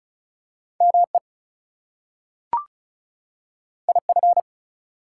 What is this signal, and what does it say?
Morse code, heard as a keyed tone, about 36 wpm, 700 Hz. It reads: ME IF
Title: Two Letter Words - 35wpm